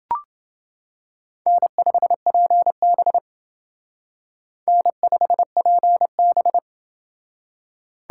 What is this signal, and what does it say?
Morse code, heard as a keyed tone, about 28 wpm, 700 Hz. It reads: N5PB N5PB